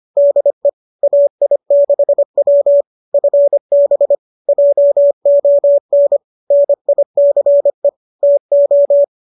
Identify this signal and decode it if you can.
Morse code, heard as a keyed tone, about 25 wpm, 570 Hz. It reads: DE AI6W FB JON NICE TO